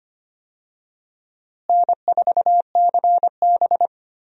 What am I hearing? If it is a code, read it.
Morse code, heard as a keyed tone, about 25 wpm, 700 Hz. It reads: N4CB
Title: USA Call Signs - 4 Characters - Repeated 3x - 25wpm